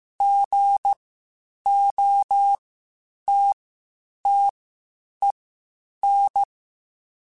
Morse code, heard as a keyed tone, about 15 wpm, spaced as for 8 wpm, 800 Hz. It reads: GOTTEN